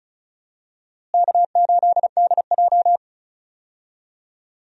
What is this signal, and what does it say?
Morse code, heard as a keyed tone, about 36 wpm, 700 Hz. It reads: K8DJ
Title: USA Call Signs - 4 Characters - Repeated 2x - 35wpm